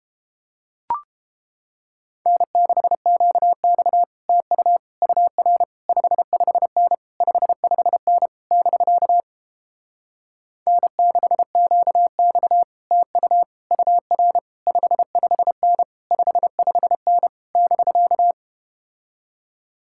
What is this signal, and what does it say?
Morse code, heard as a keyed tone, about 32 wpm, 700 Hz. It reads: N6QX TU UR 55N 55N <BK> N6QX TU UR 55N 55N <BK>